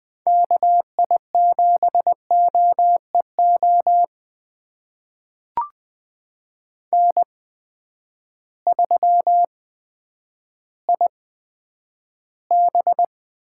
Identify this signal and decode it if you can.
Morse code, heard as a keyed tone, about 20 wpm, spaced as for 10 wpm, 700 Hz. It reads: KI7OEO N3IB
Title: USA Call Signs - 8x Character Spacing - 20wpm